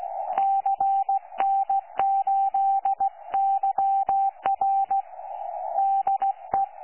Morse code, heard as a keyed tone, about 16 wpm, 780 Hz. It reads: CN8YR DE
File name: 12 45Z 18088.1kHz